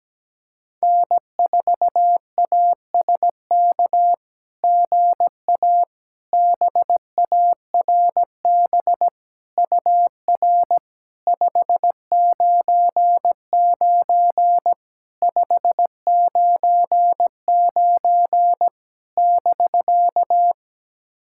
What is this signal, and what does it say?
Morse code, heard as a keyed tone, about 17 wpm, 700 Hz. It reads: N4ASK GA BARB UR 599 599 <BK>
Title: SOTA QSO Practice 17wpm